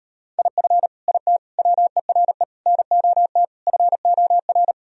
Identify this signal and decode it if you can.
Morse code, heard as a keyed tone, about 39 wpm, 700 Hz. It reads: IF IT WERE NOT FOR